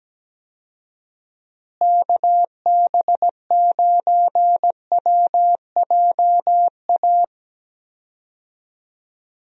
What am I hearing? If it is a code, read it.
Morse code, heard as a keyed tone, about 17 wpm, 700 Hz. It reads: KB9WJA
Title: USA Call Signs - 6 Characters - Repeated 3x - 17wpm